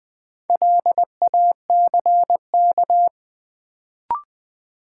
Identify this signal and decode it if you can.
Morse code, heard as a keyed tone, about 20 wpm, 700 Hz. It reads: LACK
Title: Sets of 3 Words - New Word Drill - 20wpm